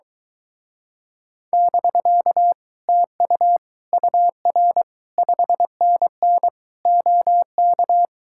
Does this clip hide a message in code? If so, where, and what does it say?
Yes: Morse code from 1.5 to 8.1 s, heard as a keyed tone, about 23 wpm, 700 Hz. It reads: <BK> TU UR 5NN OK